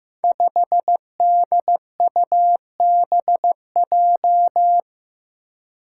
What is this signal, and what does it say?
Morse code, heard as a keyed tone, about 15 wpm, 700 Hz. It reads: 5DUBJ
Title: Five Challenging Letter-Number Combinations 15wpm